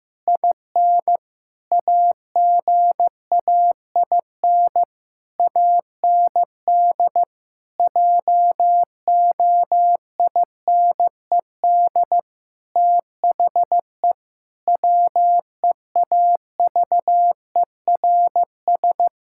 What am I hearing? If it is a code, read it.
Morse code, heard as a keyed tone, about 16 wpm, 700 Hz. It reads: IN AGAIN AND JOINED THE WEAVERS
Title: Pangram Challenge v2 - 15wpm